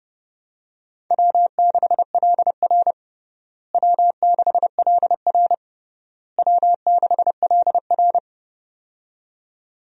Morse code, heard as a keyed tone, about 30 wpm, 700 Hz. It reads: W6LR W6LR W6LR
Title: USA Call Signs - 4 Characters - Repeated 3x - 30wpm